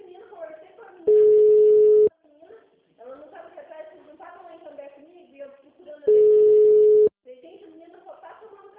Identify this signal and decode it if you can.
Ringback tone (Europe, 425 Hz), with speech.